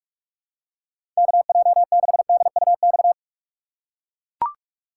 Morse code, heard as a keyed tone, about 43 wpm, 700 Hz. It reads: KJ6DUX
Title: USA Call Signs - 6 Characters - 4x Character Spacing - 45wpm